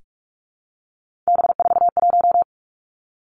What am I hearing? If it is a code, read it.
Morse code, heard as a keyed tone, about 46 wpm, 700 Hz. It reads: B41
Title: Three Challenging Letter-Number Combinations 45wpm